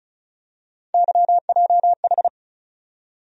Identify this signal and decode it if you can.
Morse code, heard as a keyed tone, about 36 wpm, 700 Hz. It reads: YJH
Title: USA Call Sign Postfixes - 3 Letters 35wpm